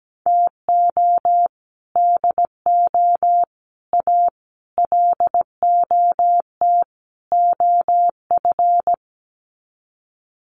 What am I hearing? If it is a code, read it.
Morse code, heard as a keyed tone, about 17 wpm, 700 Hz. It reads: TO DO A LOT OF
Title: Sets of 5 Words Encore 17wpm